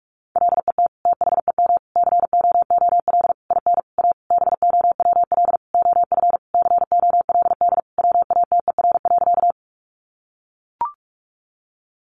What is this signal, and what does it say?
Morse code, heard as a keyed tone, about 44 wpm, 700 Hz. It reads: LET THEM COOL IN A BOWL OF COLD WATER.
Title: Sentences from Top 1200 Words 45wpm